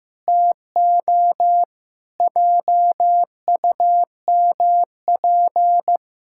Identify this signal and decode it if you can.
Morse code, heard as a keyed tone, about 15 wpm, 700 Hz. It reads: TO JUMP
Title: Pangram Challenge v2 - 15wpm